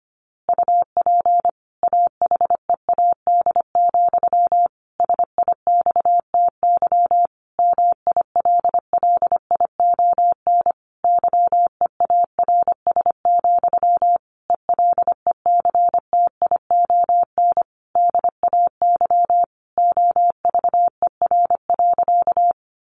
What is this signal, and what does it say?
Morse code, heard as a keyed tone, about 25 wpm, 700 Hz. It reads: UP AHEAD, SIXTY MILLION YEARS, ELECTION DAY OVER.
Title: A Sound Of Thunder 25wpm